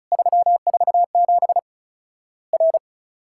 Morse code, heard as a keyed tone, about 36 wpm, 700 Hz. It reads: 347 R
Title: Worked All Europe DX Contest - QTC Exchange - 35wpm